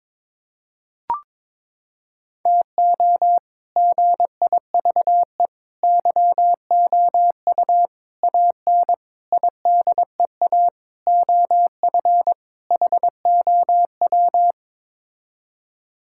Morse code, heard as a keyed tone, about 22 wpm, 700 Hz. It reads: TO GIVE YOU AN IDEA OF HOW
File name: practice / 22wpm / Sets of 7 Words - 22wpm